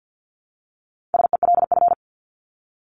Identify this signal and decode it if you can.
Morse code, heard as a keyed tone, about 48 wpm, 700 Hz. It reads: SELF
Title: Top 300 Words Repeated 50wpm